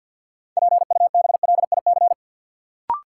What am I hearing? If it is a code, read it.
Morse code, heard as a keyed tone, about 51 wpm, 700 Hz. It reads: PUBLIC
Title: Sets of 2 Words - New Word Drill - 50wpm